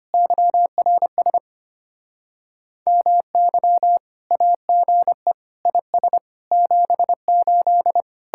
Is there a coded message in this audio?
Yes: Morse code from 0.1 to 8.0 s, heard as a keyed tone, about 33 wpm, 700 Hz. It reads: YRS MY AGE IS 78